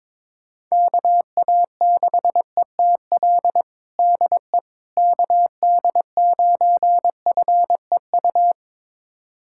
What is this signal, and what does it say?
Morse code, heard as a keyed tone, about 22 wpm, 700 Hz. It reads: KA6ETL DE KD9FEU
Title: USA Callsign DE Callsign - 6 Characters - 22wpm